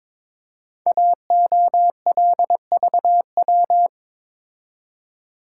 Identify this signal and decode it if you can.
Morse code, heard as a keyed tone, about 22 wpm, 700 Hz. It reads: AOLVW
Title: Code Groups - 5 Letters - 22wpm